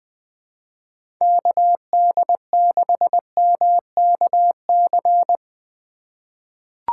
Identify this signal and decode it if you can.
Morse code, heard as a keyed tone, about 20 wpm, 700 Hz. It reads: KD6MKC
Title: USA Call Signs - 6 Characters - 3x Character Spacing - 20wpm